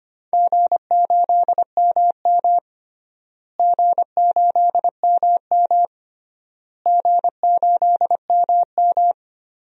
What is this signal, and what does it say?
Morse code, heard as a keyed tone, about 25 wpm, 700 Hz. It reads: G8MM G8MM G8MM
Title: DXCC-100 Call Signs - 4 Characters - Repeated 3x - 25wpm